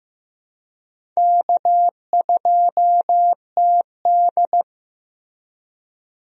Morse code, heard as a keyed tone, about 16 wpm, 700 Hz. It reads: K2TD
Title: USA Call Signs - 4 Characters - Repeated 2x - 15wpm